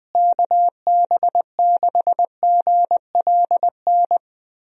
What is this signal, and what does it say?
Morse code, heard as a keyed tone, about 20 wpm, 700 Hz. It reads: KB6GLN
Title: USA Call Signs - 6 Characters - Repeated 3x - 20wpm